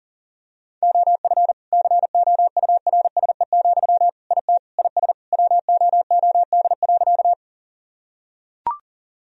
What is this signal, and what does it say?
Morse code, heard as a keyed tone, about 37 wpm, 700 Hz. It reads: OF COURSE, IT IS WOOD.
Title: Sentences from Top 1000 Words - Review Part 2 - 40wpm